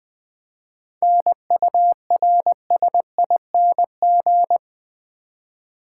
Morse code, heard as a keyed tone, about 20 wpm, 700 Hz. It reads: NURSING
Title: 7 Letter Words - First 3 Letters Given - 20wpm